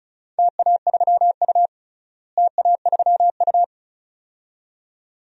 Morse code, heard as a keyed tone, about 36 wpm, 700 Hz. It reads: TA3U TA3U
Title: DXCC-100 Call Signs - 4 Characters - Repeated 2x - 35wpm